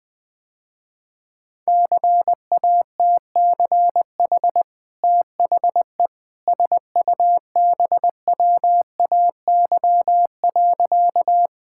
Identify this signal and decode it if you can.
Morse code, heard as a keyed tone, about 20 wpm, 700 Hz. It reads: CATCH THE SUBWAY.
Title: Easy Sentences with Hints - 20wpm